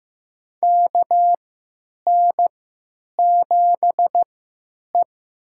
Morse code, heard as a keyed tone, about 15 wpm, spaced as for 9 wpm, 700 Hz. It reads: KN7E
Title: USA Call Signs - 4 Characters - 3x Character Spacing - 15wpm